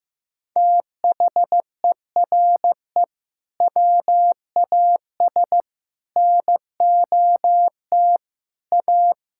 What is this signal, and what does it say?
Morse code, heard as a keyed tone, about 15 wpm, 700 Hz. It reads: THERE WAS NOT A
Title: Sentences from Top 1600 Words 15wpm